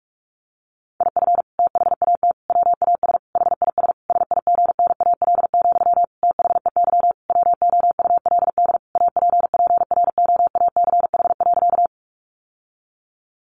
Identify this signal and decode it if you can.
Morse code, heard as a keyed tone, about 44 wpm, 700 Hz. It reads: IF THAT WAS HIS SIGNAL, THEY WOULD APPROACH.
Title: Sentences from Top 1400 Words 45wpm